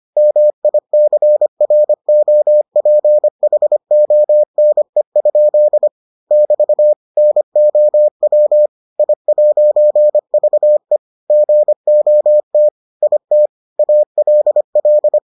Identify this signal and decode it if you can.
Morse code, heard as a keyed tone, about 25 wpm, 600 Hz. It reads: MICROPHONE? <BT> NOW I'VE GOT IT ALL